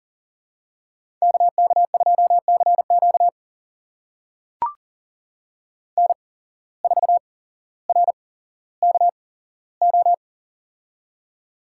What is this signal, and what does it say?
Morse code, heard as a keyed tone, about 42 wpm, spaced as for 17 wpm, 700 Hz. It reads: KK2CQ N4RKO
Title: USA Call Signs - 5 Characters - 8x Character Spacing - 40wpm